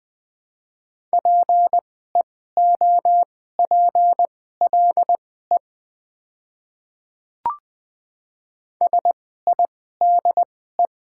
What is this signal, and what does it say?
Morse code, heard as a keyed tone, about 20 wpm, spaced as for 15 wpm, 690 Hz. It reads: PEOPLE SIDE
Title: 2x Character Spacing - Top 200 Words - 20wpm